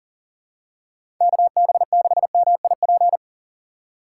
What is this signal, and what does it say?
Morse code, heard as a keyed tone, about 40 wpm, 700 Hz. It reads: KB6MIP
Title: USA Call Signs - 6 Characters - 6x Character Spacing - 40wpm